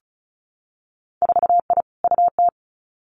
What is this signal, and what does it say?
Morse code, heard as a keyed tone, about 36 wpm, 700 Hz. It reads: 4I UT
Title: Winter Field Day Exchanges 35wpm